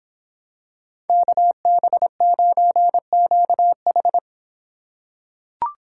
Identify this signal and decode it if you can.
Morse code, heard as a keyed tone, about 26 wpm, 700 Hz. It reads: KB9QH